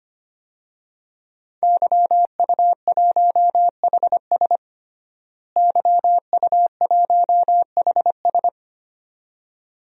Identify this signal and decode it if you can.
Morse code, heard as a keyed tone, about 25 wpm, 700 Hz. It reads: YU1HS YU1HS
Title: DXCC-100 Call Signs - 5 Characters - Repeated 2x - 25wpm